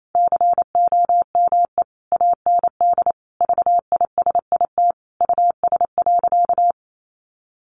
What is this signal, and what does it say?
Morse code, heard as a keyed tone, about 28 wpm, 700 Hz. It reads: COME AND VISIT US.
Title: Sentences from Top 1400 Words 28wpm